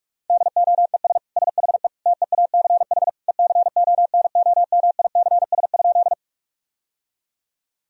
Morse code, heard as a keyed tone, about 46 wpm, 700 Hz. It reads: DOES SHE TEACH ECONOMICS?
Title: Sentences from Top 1000 Words - Review Part 3 - 45wpm